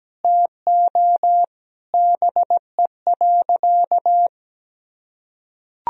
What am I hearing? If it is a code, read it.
Morse code, heard as a keyed tone, about 17 wpm, 700 Hz. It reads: TO BE.
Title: Sentences from Top 1000 Words - Review Part 3 - 17wpm